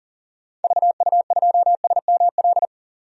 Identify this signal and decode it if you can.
Morse code, heard as a keyed tone, about 39 wpm, 700 Hz. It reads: VU2SML
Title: DXCC-100 Call Signs - 6 Characters - 40wpm